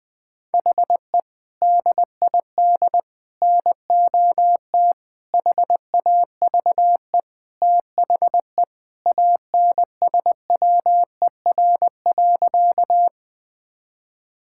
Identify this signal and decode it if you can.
Morse code, heard as a keyed tone, about 20 wpm, 700 Hz. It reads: HE DID NOT HAVE THE ANSWER.
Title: Sentences with 4-Word Phrases - Spoken Beginning - 20wpm